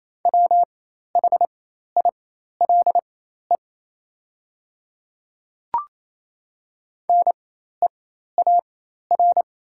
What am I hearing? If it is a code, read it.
Morse code, heard as a keyed tone, about 28 wpm, spaced as for 13 wpm, 700 Hz. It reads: WHILE NEAR
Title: 4x Character Spacing - Top 500 Words - 28wpm